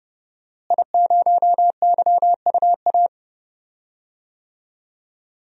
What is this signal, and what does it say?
Morse code, heard as a keyed tone, about 30 wpm, 700 Hz. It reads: I0YUA
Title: DXCC-100 Call Signs - 5 Characters - Repeated 3x - 30wpm